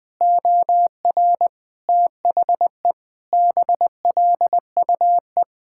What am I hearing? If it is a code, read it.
Morse code, heard as a keyed tone, about 20 wpm, 700 Hz. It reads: OR THE BLUE
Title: Sentences with 2-Word Phrases - Spoken Beginning - 20wpm